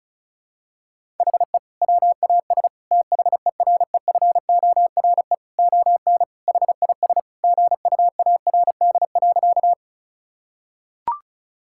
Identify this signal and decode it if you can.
Morse code, heard as a keyed tone, about 36 wpm, 700 Hz. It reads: HE WAS THEREFORE ON HIS GUARD.
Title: Sentences from Top 1000 Words - Review Part 5 - 35wpm